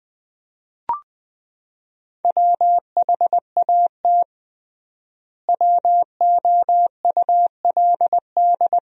Morse code, heard as a keyed tone, about 20 wpm, 700 Hz. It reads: WHAT WOULD
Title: 3x Word-Spacing - Sentences from Top 200 Words - 20wpm